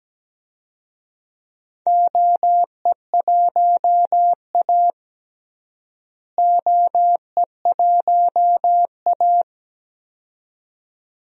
Morse code, heard as a keyed tone, about 17 wpm, 700 Hz. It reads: OE1A OE1A
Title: DXCC-100 Call Signs - 4 Characters - Repeated 2x - 17wpm